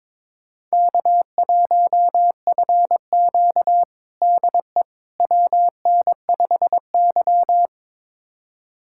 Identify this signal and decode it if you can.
Morse code, heard as a keyed tone, about 22 wpm, 700 Hz. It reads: K1FQ DE WN5Y
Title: USA Callsign DE Callsign - 4 Characters - 22wpm